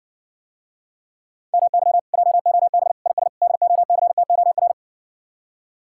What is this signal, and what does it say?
Morse code, heard as a keyed tone, about 60 wpm, 700 Hz. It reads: MY JOB IS DOCTOR